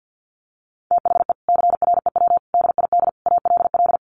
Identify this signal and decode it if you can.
Morse code, heard as a keyed tone, about 50 wpm, 700 Hz. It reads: THE CREW DID ALL
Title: Sentences from Top 1500 Words 50wpm